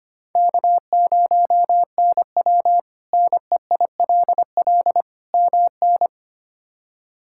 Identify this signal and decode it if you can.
Morse code, heard as a keyed tone, about 25 wpm, 700 Hz. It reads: K0NW NEILL MN
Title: K1USN SST QSO Practice - 25wpm